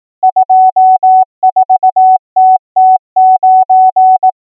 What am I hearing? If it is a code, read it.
Morse code, heard as a keyed tone, about 18 wpm, 750 Hz. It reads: 24TT9
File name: numbers / V2-morse